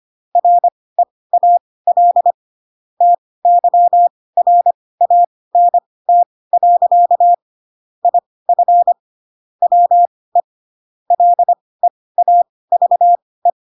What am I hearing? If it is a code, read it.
Morse code, heard as a keyed tone, about 25 wpm, spaced as for 18 wpm, 700 Hz. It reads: REAL TYRANT. IF WE LEAVE
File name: Pinocchio ch3 12 25 ik1hge